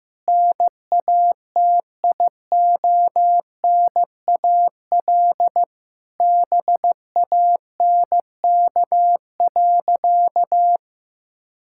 Morse code, heard as a keyed tone, about 15 wpm, 700 Hz. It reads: NATIONAL BANK.
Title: Sentences from Top 1000 Words - Review Part 6 - 15wpm